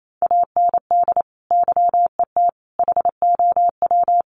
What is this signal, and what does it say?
Morse code, heard as a keyed tone, about 28 wpm, 700 Hz. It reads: AND YET HOW
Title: Sentences from Top 1300 Words 28wpm